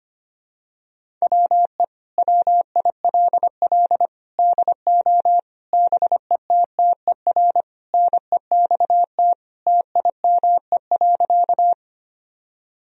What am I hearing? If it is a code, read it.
Morse code, heard as a keyed tone, about 25 wpm, 700 Hz. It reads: WE WILL DO BETTER NEXT TIME.
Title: Sentences from Top 1000 Words - Review Part 1 - 25wpm